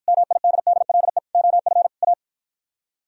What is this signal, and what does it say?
Morse code, heard as a keyed tone, about 53 wpm, 700 Hz. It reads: MIDDLE OF A